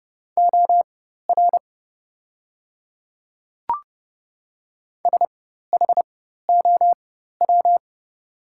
Morse code, heard as a keyed tone, about 30 wpm, spaced as for 14 wpm, 690 Hz. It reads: OR SHOW
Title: 4x Character Spacing - Top 300 Words - 30wpm